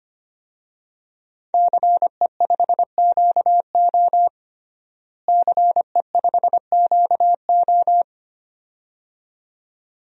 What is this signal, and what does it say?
Morse code, heard as a keyed tone, about 25 wpm, 700 Hz. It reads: CE5QO CE5QO
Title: DXCC-100 Call Signs - 5 Characters - Repeated 2x - 25wpm